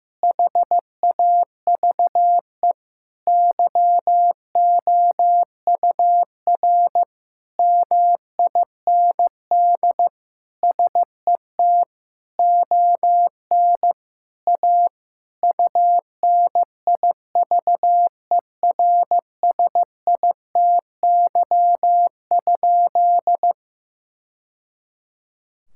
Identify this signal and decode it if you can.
Morse code, heard as a keyed tone, about 15 wpm, 700 Hz. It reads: HAVE YOUR MIND SET ON A UNIVERSITY?
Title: Sentences from Top 1000 Words - Review Part 5 - 15wpm